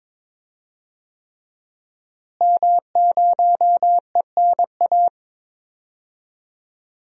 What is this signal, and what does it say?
Morse code, heard as a keyed tone, about 21 wpm, 700 Hz. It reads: M0ENA
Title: DXCC-100 Call Signs - 5 Characters - Repeated 3x - 22wpm